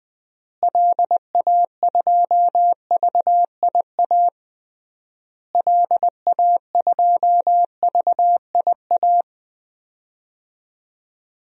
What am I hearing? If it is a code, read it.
Morse code, heard as a keyed tone, about 20 wpm, 700 Hz. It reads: LA2VIA LA2VIA
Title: DXCC-100 Call Signs - 6 Characters - Repeated 3x - 20wpm